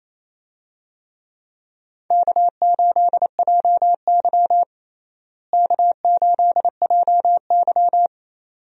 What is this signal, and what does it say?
Morse code, heard as a keyed tone, about 28 wpm, 700 Hz. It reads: K8JY K8JY
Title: USA Call Signs - 4 Characters - Repeated 2x - 28wpm